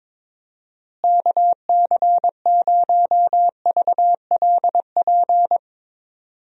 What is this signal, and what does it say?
Morse code, heard as a keyed tone, about 22 wpm, 700 Hz. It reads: KC0VLP
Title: USA Call Signs - 6 Characters - Repeated 2x - 22wpm